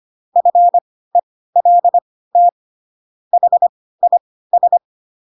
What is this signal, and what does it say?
Morse code, heard as a keyed tone, about 25 wpm, spaced as for 16 wpm, 700 Hz. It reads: FELT HIS
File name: Pinocchio / Pinocchio ch3 10 25 ik1hge